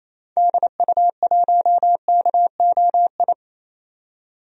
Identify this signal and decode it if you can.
Morse code, heard as a keyed tone, about 28 wpm, 700 Hz. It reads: DU1KOI
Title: DXCC-100 Call Signs - 6 Characters - 28wpm